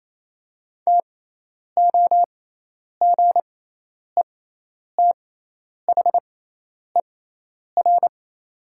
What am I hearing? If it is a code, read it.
Morse code, heard as a keyed tone, about 28 wpm, spaced as for 10 wpm, 700 Hz. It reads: TOGETHER
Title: 6x Character Spacing - Top 500 Words - 28wpm